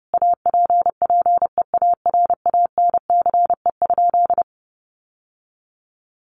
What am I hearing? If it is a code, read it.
Morse code, heard as a keyed tone, about 30 wpm, 700 Hz. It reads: APPEARANCE?
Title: Sentences from Top 1100 Words 30wpm